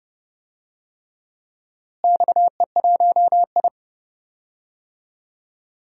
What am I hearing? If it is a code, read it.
Morse code, heard as a keyed tone, about 32 wpm, 700 Hz. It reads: XE1I
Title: DXCC-100 Call Signs - 4 Characters - Repeated 3x - 30wpm